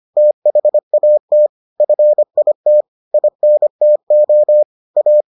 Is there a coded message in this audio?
Yes: Morse code from 0.2 to 5.2 s, heard as a keyed tone, about 25 wpm, 600 Hz. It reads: THAT FIT INTO A